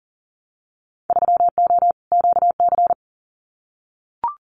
Morse code, heard as a keyed tone, about 42 wpm, 700 Hz. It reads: 3O QC
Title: Winter Field Day Exchanges Encore 40wpm